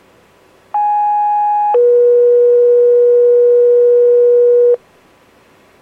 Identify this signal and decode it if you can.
Two-tone page, heard as a one-second tone, then a three-second tone, about 830 then 480 Hz. It is heard from 0.7 to 4.8 s.